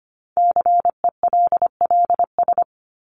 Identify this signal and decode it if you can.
Morse code, heard as a keyed tone, about 25 wpm, 700 Hz. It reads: CELLS
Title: ES:S Suffix - Top 100 Words with Base Words 25wpm